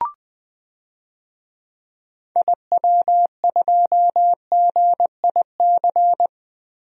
Morse code, heard as a keyed tone, about 20 wpm, 700 Hz. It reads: IW2GIC
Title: DXCC-100 Call Signs - 6 Characters - Repeated 2x - 20wpm